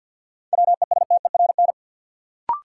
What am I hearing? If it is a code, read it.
Morse code, heard as a keyed tone, about 52 wpm, 700 Hz. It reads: WESTERN
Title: Sentences from Top 1000 Words - Review Part 7 - New Word Drill - 50wpm